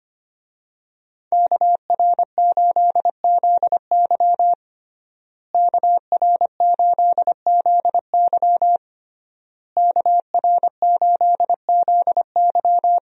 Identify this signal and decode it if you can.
Morse code, heard as a keyed tone, about 25 wpm, 700 Hz. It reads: KR8ZY KR8ZY KR8ZY
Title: USA Call Signs - 5 Characters - Repeated 3x - 25wpm